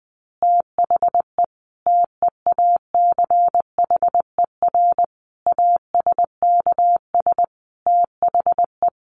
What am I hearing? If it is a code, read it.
Morse code, heard as a keyed tone, about 20 wpm, 700 Hz. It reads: THE TEACHER ASKS THE
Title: Additional Easy Phrases 20wpm